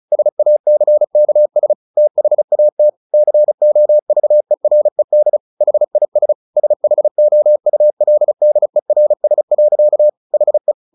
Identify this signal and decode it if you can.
Morse code, heard as a keyed tone, about 34 wpm, 600 Hz. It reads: SACKS THAT COVERED HIS SHOULDERS. HE
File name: the old man and the sea / the old man and the sea 1 35wpm